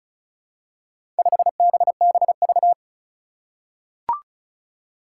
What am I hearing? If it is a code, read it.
Morse code, heard as a keyed tone, about 35 wpm, 700 Hz. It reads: 5BBV
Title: Four Challenging Letter-Number Combinations 35wpm